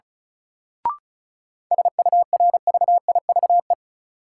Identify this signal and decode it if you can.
Morse code, heard as a keyed tone, about 36 wpm, 700 Hz. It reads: SURVIVE
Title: Sentences from Top 1300 Words - New Word Drill - 35wpm